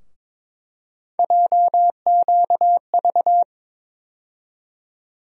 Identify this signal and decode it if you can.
Morse code, heard as a keyed tone, about 22 wpm, 700 Hz. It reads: JQV
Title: USA Call Sign Postfixes - 3 Letters 22wpm